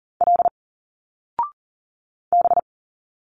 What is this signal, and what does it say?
Morse code, heard as a keyed tone, about 39 wpm, 700 Hz. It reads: L B
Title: Single Letters 40wpm